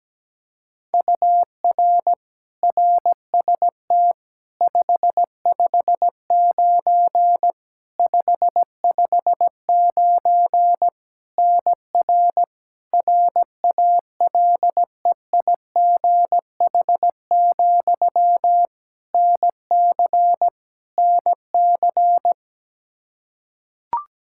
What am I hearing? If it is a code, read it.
Morse code, heard as a keyed tone, about 17 wpm, 700 Hz. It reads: UR RST 559 559 NR RALEIGH, NC NC
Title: K1USN SST QSO Practice 17wpm